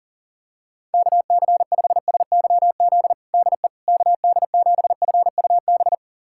Morse code, heard as a keyed tone, about 40 wpm, 700 Hz. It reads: KC5SYZ DE KD7FUB
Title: USA Callsign DE Callsign - 6 Characters - 40wpm